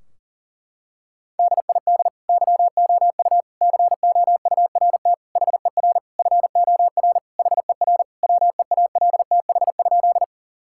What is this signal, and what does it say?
Morse code, heard as a keyed tone, about 40 wpm, 700 Hz. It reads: DID YOU COURT HER FOR HER WEALTH?
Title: Sentences from Top 1600 Words 40wpm